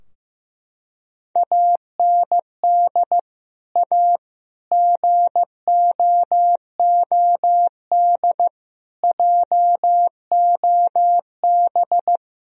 Morse code, heard as a keyed tone, about 15 wpm, 700 Hz. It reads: AND A GOOD JOB
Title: Sentences from Top 1000 Words - Review Part 3 - 15wpm